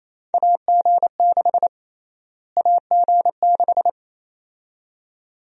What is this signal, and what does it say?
Morse code, heard as a keyed tone, about 27 wpm, 700 Hz. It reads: AG6 AG6
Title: USA Call Sign Prefixes - Repeated 2x - 28wpm